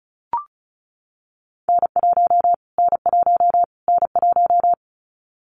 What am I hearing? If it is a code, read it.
Morse code, heard as a keyed tone, about 34 wpm, 700 Hz. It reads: N1 N1 N1
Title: Callsign Prefixes for Top 25 DXCC - Repeated 3x - 35wpm